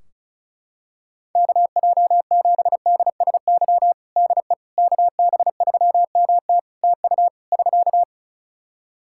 Morse code, heard as a keyed tone, about 36 wpm, 700 Hz. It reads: KJ7DSY DE KB3MT TU <SK>
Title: K1USN SST QSO Practice 35wpm